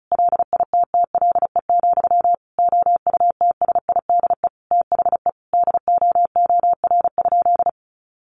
Morse code, heard as a keyed tone, about 34 wpm, 700 Hz. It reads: LITTLE, OUTSIDE THE DOOR?
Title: Sentences from Top 900 Words 35wpm